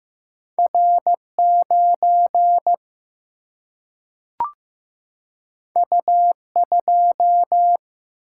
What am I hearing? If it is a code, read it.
Morse code, heard as a keyed tone, about 15 wpm, 700 Hz. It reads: R9 U2
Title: Single-Letter Single-Number 15wpm